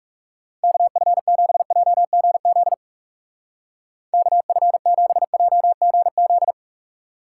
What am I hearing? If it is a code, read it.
Morse code, heard as a keyed tone, about 37 wpm, 700 Hz. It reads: KF7JGZ KF7JGZ
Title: Speed Racing - USA 2x3 Call Signs 30wpm